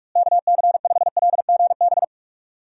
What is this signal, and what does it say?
Morse code, heard as a keyed tone, about 41 wpm, 700 Hz. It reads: KC5LGB
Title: USA Call Signs - 6 Characters - Repeated 3x - 45wpm